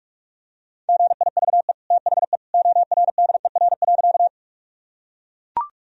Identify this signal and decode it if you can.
Morse code, heard as a keyed tone, about 41 wpm, 700 Hz. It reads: GIVE THE ORDER.
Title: Sentences from Top 1000 Words - Review Part 2 - 45wpm